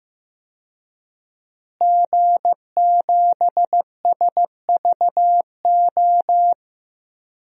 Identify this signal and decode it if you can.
Morse code, heard as a keyed tone, about 15 wpm, 700 Hz. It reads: G7SVO
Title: DXCC-100 Call Signs - 5 Characters - Repeated 3x - 15wpm